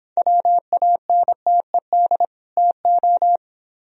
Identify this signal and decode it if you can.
Morse code, heard as a keyed tone, about 26 wpm, 700 Hz. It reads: WANTED TO